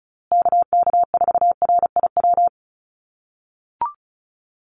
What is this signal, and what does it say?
Morse code, heard as a keyed tone, about 34 wpm, 700 Hz. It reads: KK4RIW
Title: USA Call Signs Repeated 3x 35wpm